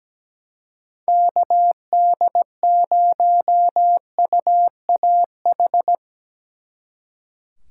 Morse code, heard as a keyed tone, about 19 wpm, 700 Hz. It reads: KD0UAH
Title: Speed Racing - USA Call Signs 17wpm